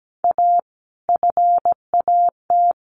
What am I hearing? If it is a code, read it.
Morse code, heard as a keyed tone, about 17 wpm, 700 Hz. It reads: A FAT
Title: Sentences from Top 1100 Words 17wpm